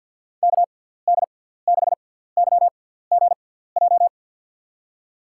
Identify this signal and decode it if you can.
Morse code, heard as a keyed tone, about 50 wpm, spaced as for 17 wpm, 700 Hz. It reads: KD6YGJ
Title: USA Call Signs - 6 Characters - 6x Character Spacing - 50wpm